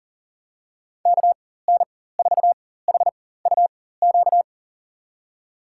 Morse code, heard as a keyed tone, about 40 wpm, spaced as for 19 wpm, 700 Hz. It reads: KN4HUQ
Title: USA Call Signs - 6 Characters - 4x Character Spacing - 40wpm